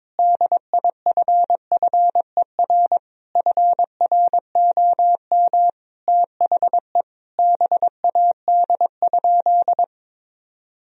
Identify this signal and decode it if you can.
Morse code, heard as a keyed tone, about 22 wpm, 700 Hz. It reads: DIFFER FROM THE BAD?
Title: Sentences from Top 1000 Words - Review Part 1 - 22wpm